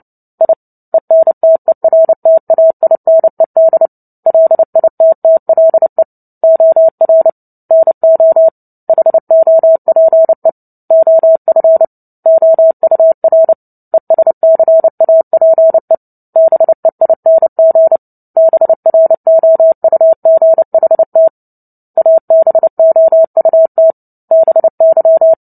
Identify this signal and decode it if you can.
Morse code, heard as a keyed tone, about 30 wpm, 650 Hz. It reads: I ENTERTAINED LITTLE OR NO HOPE OF OUR ESCAPE BEING BROUGHT ABOUT BY
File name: Book Two / 03-Chapter 3 - 29 WPM